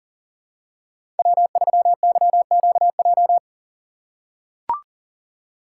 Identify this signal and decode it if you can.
Morse code, heard as a keyed tone, about 39 wpm, 700 Hz. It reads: W3YQJ
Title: Code Groups - 5 Letter:Numbers - 40wpm